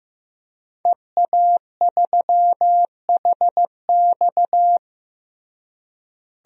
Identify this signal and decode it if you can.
Morse code, heard as a keyed tone, about 14 wpm, 700 Hz. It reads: EA3HX
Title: DXCC-100 Call Signs - 5 Characters - Repeated 3x - 15wpm